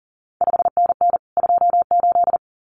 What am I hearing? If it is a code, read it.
Morse code, heard as a keyed tone, about 42 wpm, 700 Hz. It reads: 5NN 28
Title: CQ WW DX Contest Exchange Encore - 40wpm